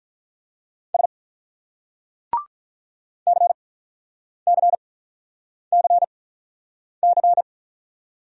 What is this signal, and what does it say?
Morse code, heard as a keyed tone, about 44 wpm, spaced as for 9 wpm, 700 Hz. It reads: S CCCC